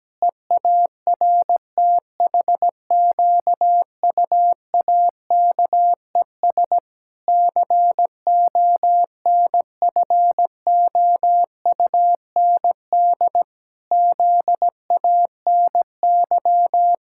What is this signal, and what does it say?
Morse code, heard as a keyed tone, about 17 wpm, 700 Hz. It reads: EARTHQUAKES CONFOUND ZANY
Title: Pangram Challenge v2 - 17wpm